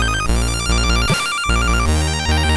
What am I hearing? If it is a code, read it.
There is no code.